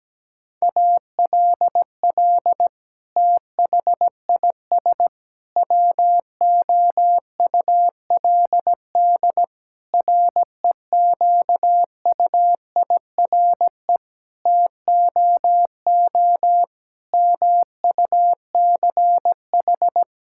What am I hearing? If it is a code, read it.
Morse code, heard as a keyed tone, about 17 wpm, 700 Hz. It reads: ALL THIS WOULD REQUIRE TOO MUCH
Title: Sentences from Top 1000 Words - Review Part 6 - 17wpm